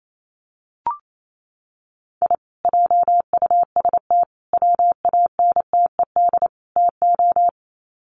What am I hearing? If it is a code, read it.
Morse code, heard as a keyed tone, about 28 wpm, 700 Hz. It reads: I JUST WANTED TO
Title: Sets of 4 Words 28wpm